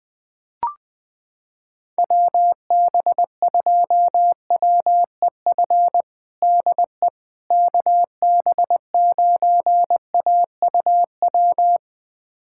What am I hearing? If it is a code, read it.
Morse code, heard as a keyed tone, about 21 wpm, 700 Hz. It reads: WB2WEF DE KB9AUW
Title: USA Callsign DE Callsign 20wpm